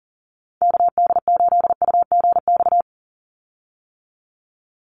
Morse code, heard as a keyed tone, about 40 wpm, 700 Hz. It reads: KD8UGX
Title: USA Call Signs 40wpm